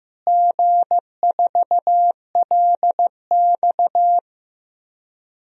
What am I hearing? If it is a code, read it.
Morse code, heard as a keyed tone, about 16 wpm, 700 Hz. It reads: G4LX
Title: DXCC-100 Call Signs - 4 Characters - Repeated 3x - 15wpm